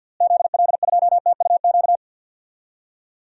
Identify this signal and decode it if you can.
Morse code, heard as a keyed tone, about 50 wpm, 700 Hz. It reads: ZL2TUQ
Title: DXCC-100 Call Signs - 6 Characters - Repeated 2x - 50wpm